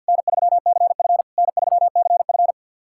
Morse code, heard as a keyed tone, about 53 wpm, 690 Hz. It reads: N3CF N3CF
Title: SOTA QSO Practice 50wpm